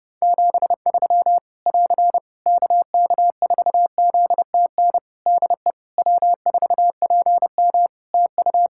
Morse code, heard as a keyed tone, about 30 wpm, 700 Hz. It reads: 73 <AR> KK4ZTN DE W4PM TU
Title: Practice QSO - Exchange 4 - Ending - 30wpm